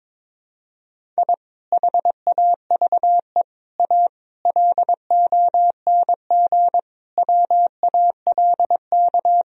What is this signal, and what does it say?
Morse code, heard as a keyed tone, about 22 wpm, 700 Hz. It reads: I HAVE A LONG WALK
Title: Sentences from Top 1000 Words - Review Part 1 - 22wpm